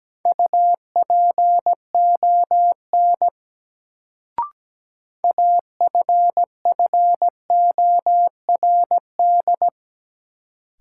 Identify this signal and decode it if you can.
Morse code, heard as a keyed tone, about 17 wpm, 700 Hz. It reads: UPON AFFORD
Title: Sentences from Top 1100 Words - New Word Drill Alt - 17wpm